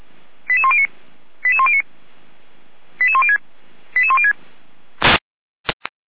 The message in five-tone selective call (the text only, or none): 88109 88109 88106 88106